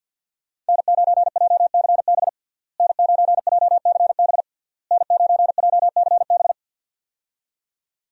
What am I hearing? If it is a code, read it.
Morse code, heard as a keyed tone, about 51 wpm, 700 Hz. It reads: N9JCB N9JCB N9JCB
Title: USA Call Signs - 5 Characters - Repeated 3x - 50wpm